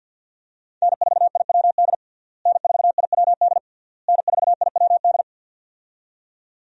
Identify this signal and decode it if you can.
Morse code, heard as a keyed tone, about 50 wpm, 700 Hz. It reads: N4IWD N4IWD N4IWD
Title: USA Call Signs - 5 Characters - Repeated 3x - 50wpm